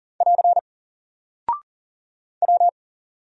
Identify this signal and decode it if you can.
Morse code, heard as a keyed tone, about 40 wpm, 700 Hz. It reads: <AR> W
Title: Single Character (Letter:Number:Punctuation:Prosign) 40wpm